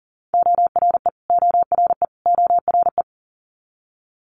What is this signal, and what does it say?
Morse code, heard as a keyed tone, about 40 wpm, 690 Hz. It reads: ORE ORE ORE
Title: Multi-letter Phonemes Repeated 3x 40wpm